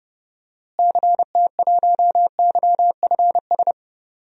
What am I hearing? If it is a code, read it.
Morse code, heard as a keyed tone, about 29 wpm, 700 Hz. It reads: CT1YFS
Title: DXCC-100 Call Signs - 6 Characters - Repeated 3x - 30wpm